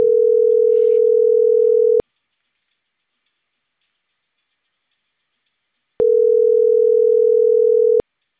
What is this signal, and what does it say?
Ringback tone (North America), 440 + 480 Hz.